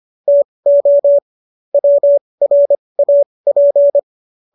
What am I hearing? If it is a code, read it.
Morse code, heard as a keyed tone, about 25 wpm, spaced as for 20 wpm, 570 Hz. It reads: TO WRAP
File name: qso221 20